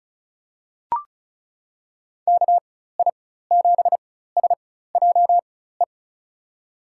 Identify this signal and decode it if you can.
Morse code, heard as a keyed tone, about 34 wpm, spaced as for 16 wpm, 700 Hz. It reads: KI7SJE
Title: USA Call Signs - 6 Characters - 4x Character Spacing - 35wpm